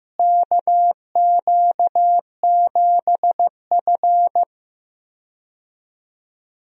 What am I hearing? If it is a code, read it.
Morse code, heard as a keyed tone, about 15 wpm, 700 Hz. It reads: KQ7F
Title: USA Call Signs - 4 Characters - Repeated 2x - 15wpm